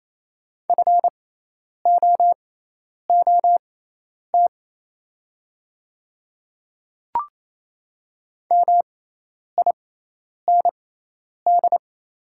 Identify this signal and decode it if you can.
Morse code, heard as a keyed tone, about 28 wpm, spaced as for 10 wpm, 700 Hz. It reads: FOOT MIND
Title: 6x Character Spacing - Top 500 Words - 28wpm